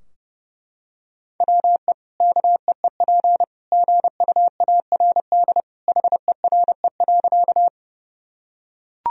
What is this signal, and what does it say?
Morse code, heard as a keyed tone, about 30 wpm, 700 Hz. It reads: WE KEEP GUARD HERE.
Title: Sentences from Top 1000 Words - Review Part 4 - 30wpm